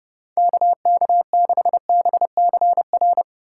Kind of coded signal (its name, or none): Morse code